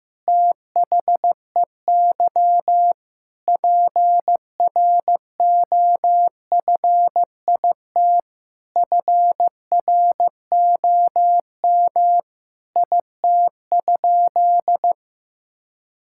Morse code, heard as a keyed tone, about 16 wpm, 700 Hz. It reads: THEY PROFIT FROM IT?
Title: Sentences from Top 1000 Words - Review Part 5 - 15wpm